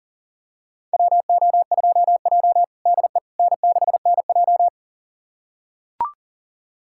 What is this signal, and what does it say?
Morse code, heard as a keyed tone, about 37 wpm, 700 Hz. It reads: WO2J DE N6NJ
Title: USA Callsign DE Callsign - 4 Characters - 40wpm